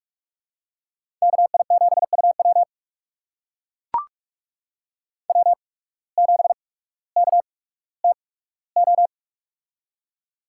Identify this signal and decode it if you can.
Morse code, heard as a keyed tone, about 44 wpm, spaced as for 19 wpm, 690 Hz. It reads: KI7UW W7KTO
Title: USA Call Signs - 5 Characters - 8x Character Spacing - 45wpm